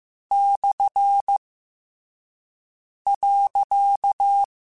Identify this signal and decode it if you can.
Morse code, heard as a keyed tone, about 15 wpm, 800 Hz. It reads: / .
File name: Misc / Misc-mp3s / Punct @13